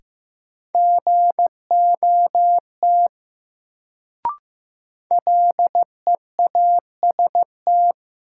Morse code, heard as a keyed tone, about 15 wpm, 700 Hz. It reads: GOT LEAST
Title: Sets of 2 Words - New Word Drill - 15wpm